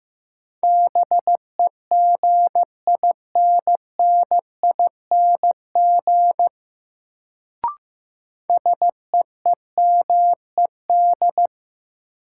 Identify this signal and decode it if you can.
Morse code, heard as a keyed tone, about 15 wpm, 700 Hz. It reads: BEGINNING SEEMED
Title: Sets of 3 Words - New Word Drill - 15wpm